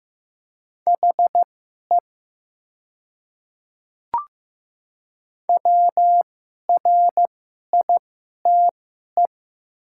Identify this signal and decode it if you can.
Morse code, heard as a keyed tone, about 15 wpm, spaced as for 11 wpm, 700 Hz. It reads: HE WRITE
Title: 2x Character Spacing - Top 200 Words - 15wpm